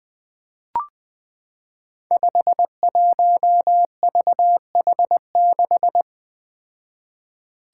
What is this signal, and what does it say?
Morse code, heard as a keyed tone, about 20 wpm, 700 Hz. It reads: 51VH6
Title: Five Challenging Letter-Number Combinations 20wpm